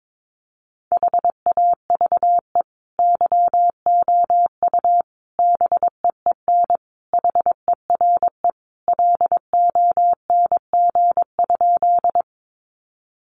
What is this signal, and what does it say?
Morse code, heard as a keyed tone, about 22 wpm, 700 Hz. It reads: HAVE YOU BEEN HERE LONG?
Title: Sentences from Top 200 Words 22wpm